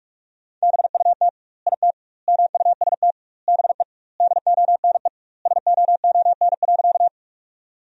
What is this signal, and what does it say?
Morse code, heard as a keyed tone, about 43 wpm, 700 Hz. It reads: BUT IT MUST BE DONE SOON.
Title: 2x Word-Spacing - Sentences from Top 500 Words - 45wpm